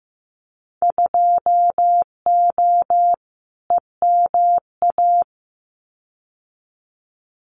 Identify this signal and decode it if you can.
Morse code, heard as a keyed tone, about 15 wpm, 700 Hz. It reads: 2O EMA
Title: Winter Field Day Exchanges 15wpm